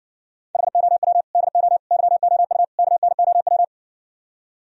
Morse code, heard as a keyed tone, about 60 wpm, 700 Hz. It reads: HOW DO YOU KNOW